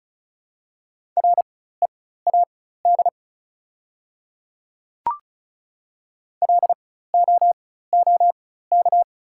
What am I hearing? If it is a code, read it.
Morse code, heard as a keyed tone, about 36 wpm, spaced as for 16 wpm, 700 Hz. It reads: READ LOOK
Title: 4x Character Spacing - Top 200 Words - 35wpm